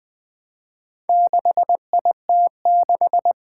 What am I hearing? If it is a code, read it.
Morse code, heard as a keyed tone, about 20 wpm, 700 Hz. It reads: 6IT6
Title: Four Challenging Letter-Number Combinations 20wpm